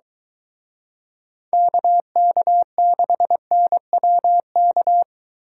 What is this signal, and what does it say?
Morse code, heard as a keyed tone, about 23 wpm, 700 Hz. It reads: KK6NWK